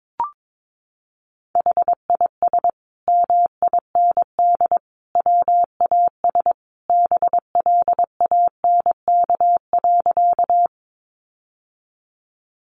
Morse code, heard as a keyed tone, about 22 wpm, 700 Hz. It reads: HIS MIND WAS BLANK.
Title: Sentences from Top 1500 Words 22wpm